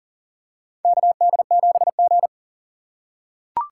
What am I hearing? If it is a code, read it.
Morse code, heard as a keyed tone, about 40 wpm, 700 Hz. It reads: KD7G
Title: USA Call Signs - 4 Characters - 3x Character Spacing - 40wpm